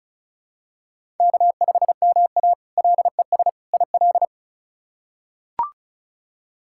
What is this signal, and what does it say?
Morse code, heard as a keyed tone, about 34 wpm, 700 Hz. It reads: K5MA LES IL
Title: K1USN SST QSO Practice - 35wpm